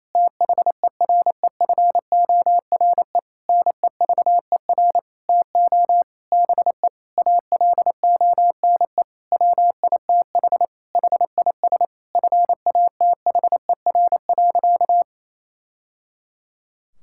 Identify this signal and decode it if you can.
Morse code, heard as a keyed tone, about 28 wpm, 700 Hz. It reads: THEREFORE NEVER TO BE ALONE WITH HIS FATHER.
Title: Sentences from Top 1000 Words - Review Part 5 - 28wpm